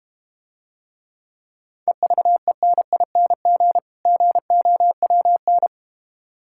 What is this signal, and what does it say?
Morse code, heard as a keyed tone, about 32 wpm, 700 Hz. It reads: EVENING GOWN